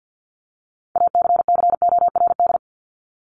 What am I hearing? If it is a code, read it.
Morse code, heard as a keyed tone, about 50 wpm, 700 Hz. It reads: ACCORD
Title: ING Suffix - Top 100 Words with Base Words 50wpm